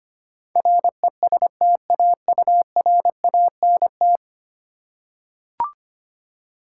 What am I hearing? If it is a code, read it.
Morse code, heard as a keyed tone, about 25 wpm, 700 Hz. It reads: RESTAURANT
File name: 10 Letter Words - First 3 Letters Given - 25wpm